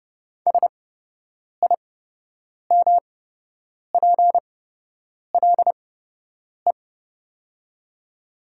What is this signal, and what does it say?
Morse code, heard as a keyed tone, about 30 wpm, spaced as for 8 wpm, 700 Hz. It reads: SIMPLE
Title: 8x Character Spacing - Top 500 Words - 30wpm